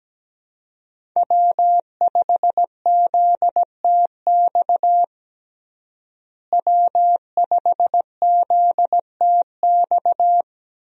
Morse code, heard as a keyed tone, about 17 wpm, 700 Hz. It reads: W5ZTX W5ZTX
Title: USA Call Signs - 5 Characters - Repeated 2x - 17wpm